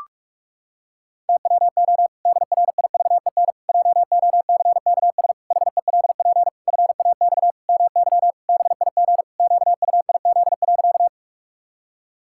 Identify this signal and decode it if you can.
Morse code, heard as a keyed tone, about 41 wpm, 700 Hz. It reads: TWO DRIVEN JOCKS HELP FAX MY BIG QUIZ.
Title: Pangram Challenge v2 - 45wpm